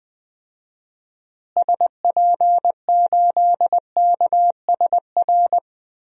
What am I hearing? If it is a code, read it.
Morse code, heard as a keyed tone, about 20 wpm, 700 Hz. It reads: SP8KSR
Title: DXCC-100 Call Signs - 6 Characters - Repeated 3x - 20wpm